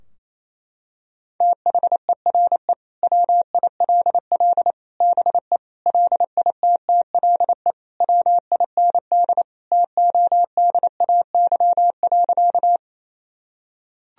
Morse code, heard as a keyed tone, about 28 wpm, 700 Hz. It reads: THERE WILL BE LITTLE WIND TODAY.
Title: Sentences from Top 1000 Words - Review Part 2 - 28wpm